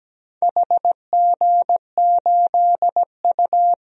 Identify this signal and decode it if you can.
Morse code, heard as a keyed tone, about 17 wpm, 700 Hz. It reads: HG8U
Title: DXCC-100 Call Signs - 4 Characters - Repeated 3x - 17wpm